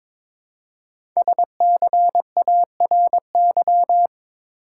Morse code, heard as a keyed tone, about 22 wpm, 700 Hz. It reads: SCARY
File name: practice / 22wpm / 5 Letter Words - First 3 Letters Given - 22wpm